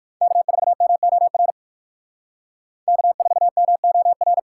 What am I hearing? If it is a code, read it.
Morse code, heard as a keyed tone, about 57 wpm, 700 Hz. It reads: K4MOR K4MOR